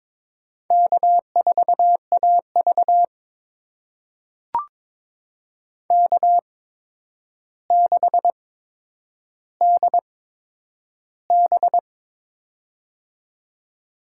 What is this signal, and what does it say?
Morse code, heard as a keyed tone, about 23 wpm, spaced as for 10 wpm, 700 Hz. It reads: K4AV K6DB
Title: USA Call Signs - 4 Characters - 8x Character Spacing - 22wpm